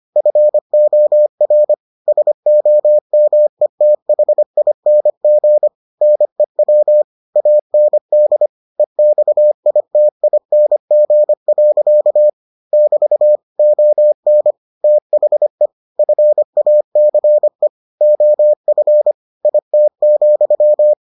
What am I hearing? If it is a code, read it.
Morse code, heard as a keyed tone, about 25 wpm, 600 Hz. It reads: FOR SOMETHING NEW AND EXITING. <BT> ON THE FACE OF IT,